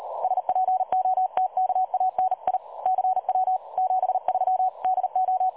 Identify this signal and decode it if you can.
Morse code, heard as a keyed tone, about 37 wpm, 750 Hz. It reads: SPOTKANI CW 73 DO
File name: dlazaawansowanych / nagraniazpasma / websdr recording 3565.0kHz